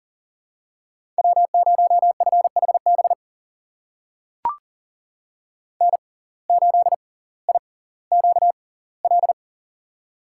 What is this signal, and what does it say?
Morse code, heard as a keyed tone, about 40 wpm, spaced as for 20 wpm, 700 Hz. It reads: W0FHB N8IQL
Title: USA Call Signs - 5 Characters - 6x Character Spacing - 40wpm